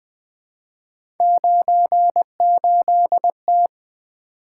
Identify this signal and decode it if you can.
Morse code, heard as a keyed tone, about 20 wpm, 700 Hz. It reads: 98T
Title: Three Letter-Number Combinations - 20wpm